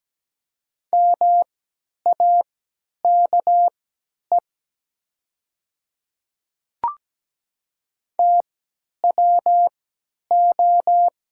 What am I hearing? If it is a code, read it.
Morse code, heard as a keyed tone, about 17 wpm, spaced as for 10 wpm, 700 Hz. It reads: MAKE TWO
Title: 3x Character Spacing - Top 100 Words - 17wpm